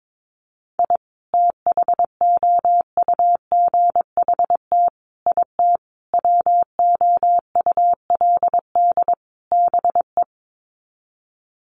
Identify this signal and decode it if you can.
Morse code, heard as a keyed tone, about 22 wpm, 700 Hz. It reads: I THOUGHT IT WOULD BE
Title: Sets of 5 Words Encore 22wpm